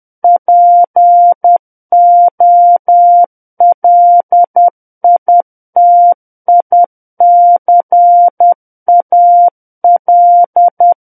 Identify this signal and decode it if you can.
Morse code, heard as a keyed tone, about 10 wpm, 700 Hz. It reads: POLITICAL
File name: Book One / 01-Chapter 1 - 10 WPM